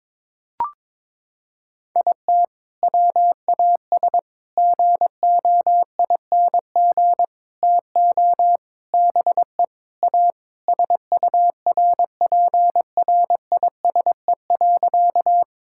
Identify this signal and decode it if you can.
Morse code, heard as a keyed tone, about 21 wpm, 700 Hz. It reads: IT WAS GOING TO BE A SURPRISE.
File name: practice / 22wpm / Sentences with 4-Word Phrases - 22wpm